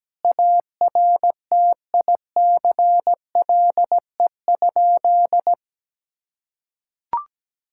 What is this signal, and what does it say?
Morse code, heard as a keyed tone, about 17 wpm, 700 Hz. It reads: ARTICLE?
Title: Sentences from Top 1000 Words - Review Part 5 - 17wpm